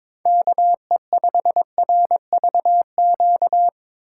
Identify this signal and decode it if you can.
Morse code, heard as a keyed tone, about 22 wpm, 700 Hz. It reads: KE5RVQ
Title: USA Call Signs - 6 Characters - 3x Character Spacing - 22wpm